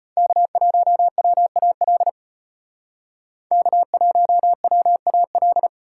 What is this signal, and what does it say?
Morse code, heard as a keyed tone, about 32 wpm, 700 Hz. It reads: K1WAL K1WAL